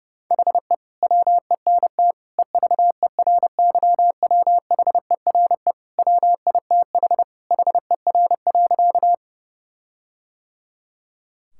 Morse code, heard as a keyed tone, about 30 wpm, 700 Hz. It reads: HE WENT EVERYWHERE WITH HER.
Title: Sentences from Top 1600 Words 30wpm